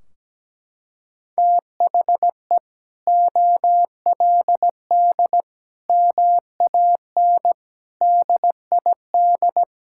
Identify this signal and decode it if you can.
Morse code, heard as a keyed tone, about 17 wpm, 700 Hz. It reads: THE OLD MAN DID
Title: Sentences from Top 1600 Words 17wpm